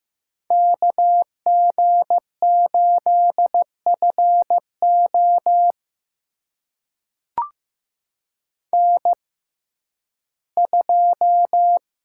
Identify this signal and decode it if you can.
Morse code, heard as a keyed tone, about 15 wpm, spaced as for 11 wpm, 700 Hz. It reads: KG8FO N2
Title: USA Call Signs - 5 Characters - 6x Character Spacing - 15wpm